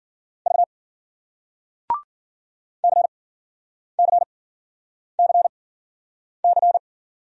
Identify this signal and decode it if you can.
Morse code, heard as a keyed tone, about 52 wpm, spaced as for 9 wpm, 700 Hz. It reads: V CCCC